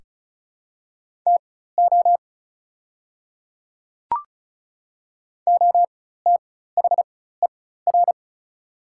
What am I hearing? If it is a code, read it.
Morse code, heard as a keyed tone, about 36 wpm, spaced as for 16 wpm, 700 Hz. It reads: TO OTHER
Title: 4x Character Spacing - Top 100 Words - 35wpm